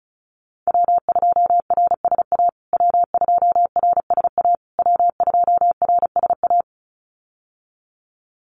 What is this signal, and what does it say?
Morse code, heard as a keyed tone, about 34 wpm, 700 Hz. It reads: W2RSA W2RSA W2RSA
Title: USA Call Signs Repeated 3x 35wpm